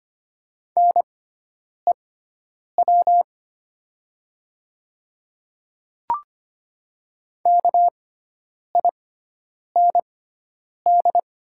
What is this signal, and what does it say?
Morse code, heard as a keyed tone, about 25 wpm, spaced as for 9 wpm, 690 Hz. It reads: NEW KIND
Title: 6x Character Spacing - Top 200 Words - 25wpm